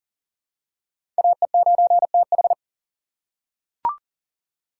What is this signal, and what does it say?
Morse code, heard as a keyed tone, about 40 wpm, 700 Hz. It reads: AE9TH
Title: USA Call Signs - 2x Character Spacing - 40wpm